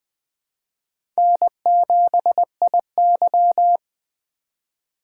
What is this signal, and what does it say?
Morse code, heard as a keyed tone, about 20 wpm, 700 Hz. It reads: N7IY
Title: USA Call Signs - 4 Characters - 3x Character Spacing - 20wpm